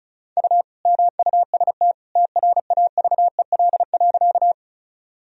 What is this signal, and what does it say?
Morse code, heard as a keyed tone, about 34 wpm, 700 Hz. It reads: U MUST TRAVEL.
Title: Sentences from Top 400 Words and Abbreviations 35wpm